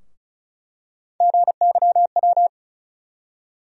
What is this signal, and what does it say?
Morse code, heard as a keyed tone, about 34 wpm, 700 Hz. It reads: GYW
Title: USA Call Sign Postfixes - 3 Letters 35wpm